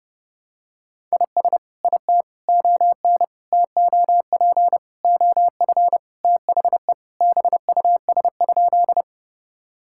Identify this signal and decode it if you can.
Morse code, heard as a keyed tone, about 30 wpm, 700 Hz. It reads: IS IT ON TOP OF THE BUS?
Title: Sentences from Top 1000 Words - Review Part 6 - 30wpm